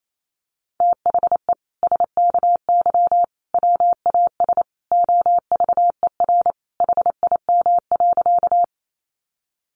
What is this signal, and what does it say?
Morse code, heard as a keyed tone, about 28 wpm, 700 Hz. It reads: THE SKY WAS OVER HIM.
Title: Sentences from Top 1200 Words 28wpm